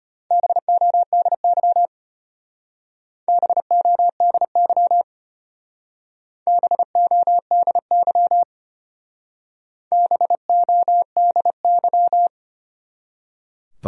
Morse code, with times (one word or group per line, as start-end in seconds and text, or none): 0.3-1.9 BODY
3.3-5.0 BODY
6.5-8.4 BODY
9.9-12.3 BODY